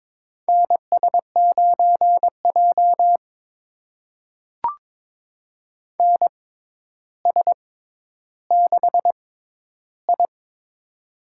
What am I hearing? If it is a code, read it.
Morse code, heard as a keyed tone, about 22 wpm, spaced as for 11 wpm, 700 Hz. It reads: NS9J NS6I
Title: USA Call Signs - 4 Characters - 6x Character Spacing - 22wpm